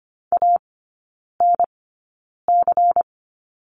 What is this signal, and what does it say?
Morse code, heard as a keyed tone, about 25 wpm, spaced as for 9 wpm, 700 Hz. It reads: ANC
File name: Lesson 17e - Letter u - Three Characters 25wpm